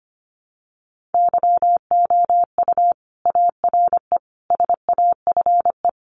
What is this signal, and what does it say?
Morse code, heard as a keyed tone, about 25 wpm, 700 Hz. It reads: YOU ARE SAFE
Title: Sentences from Top 900 Words 25wpm